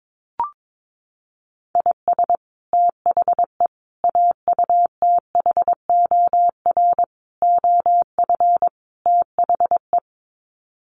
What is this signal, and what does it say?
Morse code, heard as a keyed tone, about 22 wpm, 700 Hz. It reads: IS THE AUTHOR OF THE
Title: Sets of 5 Words Encore 22wpm